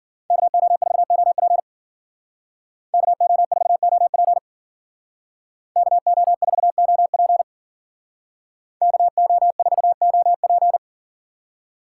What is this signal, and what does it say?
Morse code, heard as a keyed tone, about 52 wpm, 700 Hz. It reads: KO4OP KO4OP KO4OP KO4OP